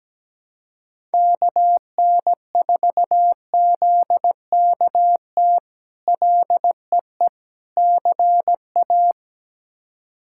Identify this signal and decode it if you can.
Morse code, heard as a keyed tone, about 17 wpm, 700 Hz. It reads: KN4ZKT LEE CA
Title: K1USN SST QSO Practice - 17wpm